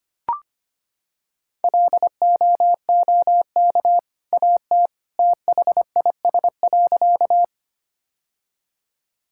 Morse code, heard as a keyed tone, about 25 wpm, 700 Hz. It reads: LOOK AT THIS.
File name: practice / 25wpm / Sentences with 2-Word Phrases - 25wpm